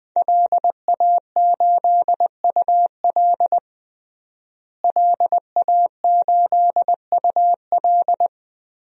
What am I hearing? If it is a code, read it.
Morse code, heard as a keyed tone, about 20 wpm, 700 Hz. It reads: LA8UL LA8UL
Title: DXCC-100 Call Signs - 5 Characters - Repeated 2x - 20wpm